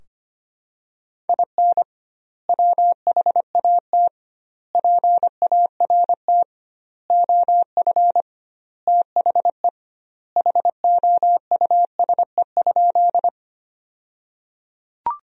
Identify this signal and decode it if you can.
Morse code, heard as a keyed tone, about 25 wpm, 700 Hz. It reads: IN WHAT PART OF THE HOUSE?
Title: 2x Word-Spacing - Sentences from Top 200 Words - 25wpm